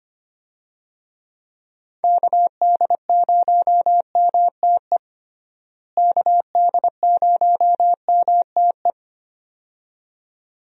Morse code, heard as a keyed tone, about 25 wpm, 700 Hz. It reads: KD0MTE KD0MTE
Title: USA Call Signs - 6 Characters - Repeated 2x - 25wpm